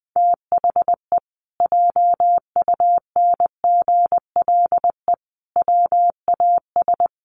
Morse code, heard as a keyed tone, about 20 wpm, 700 Hz. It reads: THE JUNGLE WAS
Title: A Sound Of Thunder 20wpm